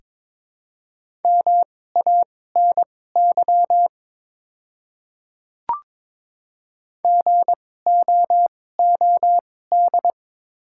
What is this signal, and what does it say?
Morse code, heard as a keyed tone, about 22 wpm, spaced as for 16 wpm, 700 Hz. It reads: MANY GOOD
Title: 2x Character Spacing - Top 200 Words - 22wpm